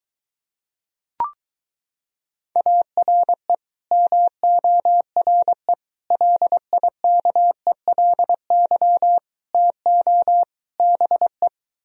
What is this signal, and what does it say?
Morse code, heard as a keyed tone, about 22 wpm, 700 Hz. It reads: ARE MORE LIKELY TO BE